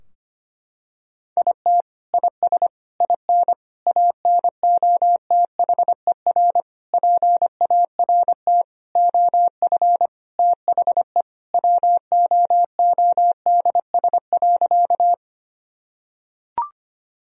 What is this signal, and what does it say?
Morse code, heard as a keyed tone, about 25 wpm, 700 Hz. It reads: IT IS IN ANOTHER PART OF THE WOODS.
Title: Sentences from Top 1000 Words - Review Part 2 - 25wpm